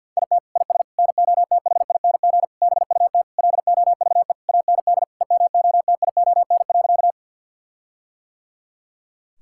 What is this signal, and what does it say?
Morse code, heard as a keyed tone, about 50 wpm, 700 Hz. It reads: IT IS NOTHING BUT LOVE AND EMOTION.
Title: Sentences from Top 1600 Words 50wpm